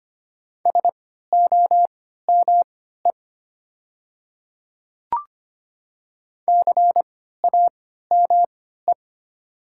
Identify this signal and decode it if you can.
Morse code, heard as a keyed tone, about 25 wpm, spaced as for 14 wpm, 700 Hz. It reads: SOME CAME
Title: 3x Character Spacing - Top 300 Words - 25wpm